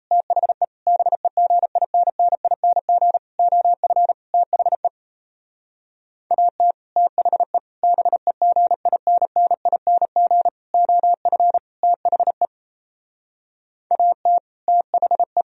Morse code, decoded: THE BEGINNING OF THE AT THE BEGINNING OF THE AT THE

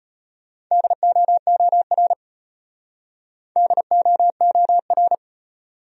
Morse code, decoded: DOOR DOOR

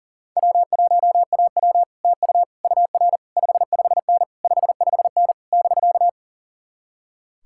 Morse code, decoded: W1AW TU UR 55N 55N <BK>

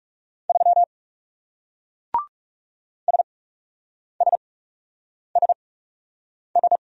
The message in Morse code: 3 SSSS

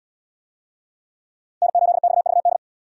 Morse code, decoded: N8XVD

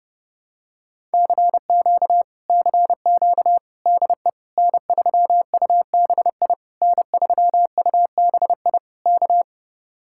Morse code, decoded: CQ CQ DE N3UBI N3UBI K

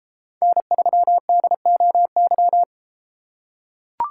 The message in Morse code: N3DOY

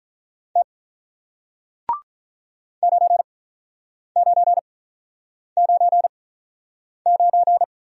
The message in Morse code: T 9 9 9 9

